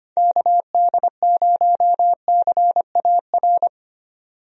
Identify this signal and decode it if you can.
Morse code, heard as a keyed tone, about 25 wpm, 700 Hz. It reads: KD0CAR
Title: USA Call Signs - 6 Characters - 25wpm